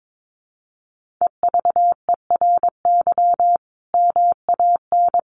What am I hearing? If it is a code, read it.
Morse code, heard as a keyed tone, about 22 wpm, 700 Hz. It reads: EVERY MAN
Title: Sentences from Top 1400 Words 22wpm